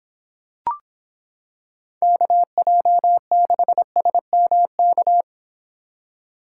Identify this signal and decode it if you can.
Morse code, heard as a keyed tone, about 26 wpm, 700 Hz. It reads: KJ6SMK